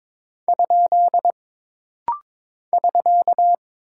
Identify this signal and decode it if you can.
Morse code, heard as a keyed tone, about 22 wpm, 700 Hz. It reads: ? <SK>
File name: practice / 22wpm / Single Punctuation-Prosign - Rapid-Fire - 22wpm